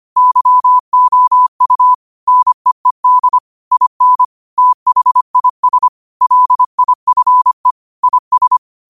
Morse code, heard as a keyed tone, about 25 wpm, 1000 Hz. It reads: YOU NEED IN THIS LIFE IS
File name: qotdfiles / 25wpm